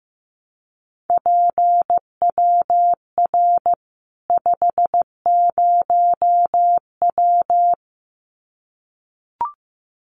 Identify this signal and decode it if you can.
Morse code, heard as a keyed tone, about 15 wpm, 700 Hz. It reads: PWR 50W
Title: Common QSO Phrases 15wpm